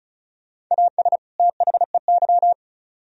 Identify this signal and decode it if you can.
Morse code, heard as a keyed tone, about 42 wpm, 700 Hz. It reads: AS THEY
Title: Speed-Racing - Sets of 2 Words 35wpm